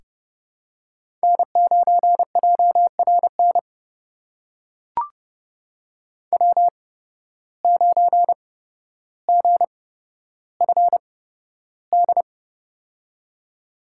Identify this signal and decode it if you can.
Morse code, heard as a keyed tone, about 30 wpm, spaced as for 13 wpm, 700 Hz. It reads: N9JRN W9GFD
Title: USA Call Signs - 5 Characters - 8x Character Spacing - 30wpm